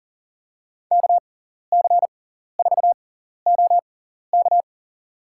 Morse code, heard as a keyed tone, about 37 wpm, spaced as for 14 wpm, 700 Hz. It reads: KC4OK
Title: USA Call Signs - 5 Characters - 6x Character Spacing - 40wpm